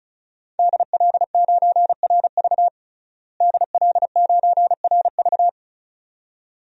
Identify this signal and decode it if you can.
Morse code, heard as a keyed tone, about 36 wpm, 700 Hz. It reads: DL9RV DL9RV
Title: DXCC-100 Call Signs - 5 Characters - Repeated 2x - 35wpm